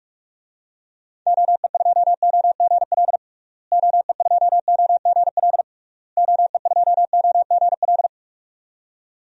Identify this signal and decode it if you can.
Morse code, heard as a keyed tone, about 46 wpm, 700 Hz. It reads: OE2OGL OE2OGL OE2OGL
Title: DXCC-100 Call Signs - 6 Characters - Repeated 3x - 45wpm